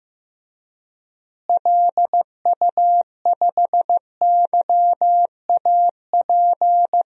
Morse code, heard as a keyed tone, about 15 wpm, 700 Hz. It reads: LU5YAP
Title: DXCC-100 Call Signs - 6 Characters - Repeated 3x - 15wpm